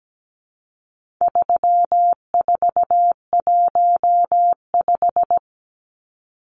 Morse code, heard as a keyed tone, about 17 wpm, 700 Hz. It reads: 3415
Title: Numbers - 4-Digit 17wpm